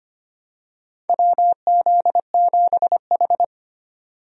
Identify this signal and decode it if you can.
Morse code, heard as a keyed tone, about 25 wpm, 700 Hz. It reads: WZ7H
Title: USA Call Signs - 4 Characters - 25wpm